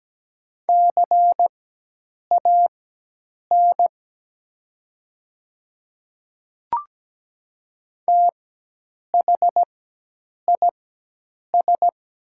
Morse code, heard as a keyed tone, about 17 wpm, spaced as for 8 wpm, 700 Hz. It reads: CAN THIS
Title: 4x Character Spacing - Top 100 Words - 17wpm